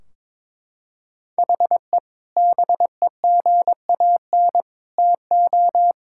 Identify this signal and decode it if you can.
Morse code, heard as a keyed tone, about 22 wpm, 700 Hz. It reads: HE BEGAN TO
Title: Sentences from Top 1600 Words 22wpm